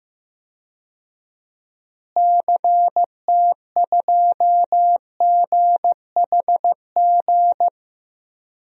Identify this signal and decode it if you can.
Morse code, heard as a keyed tone, about 15 wpm, 700 Hz. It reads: CT2GHG
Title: DXCC-100 Call Signs - 6 Characters - Repeated 3x - 15wpm